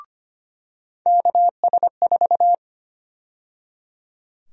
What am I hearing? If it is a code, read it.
Morse code, heard as a keyed tone, about 25 wpm, 700 Hz. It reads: KS4
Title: USA Call Sign Prefixes - 25wpm